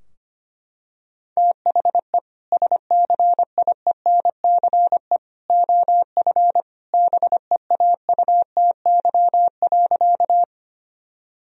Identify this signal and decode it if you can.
Morse code, heard as a keyed tone, about 25 wpm, 700 Hz. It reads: THE SCIENCE OF BEAUTY.
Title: Sentences from Top 1000 Words - Review Part 2 - 25wpm